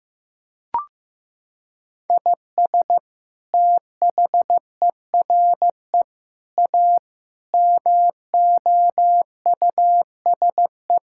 Morse code, heard as a keyed tone, about 16 wpm, 700 Hz. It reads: IS THERE A MOUSE
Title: Sentences from Top 1000 Words - Review Part 5 - 15wpm